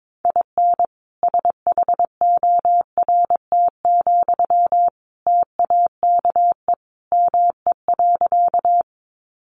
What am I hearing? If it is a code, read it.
Morse code, heard as a keyed tone, about 22 wpm, 700 Hz. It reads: IN SHORT, TAKE ME.
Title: Sentences from Top 400 Words 22wpm